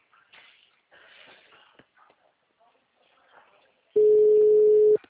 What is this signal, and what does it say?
Ringback tone (Europe, 425 Hz), with speech.
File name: out- -2016- - - .1145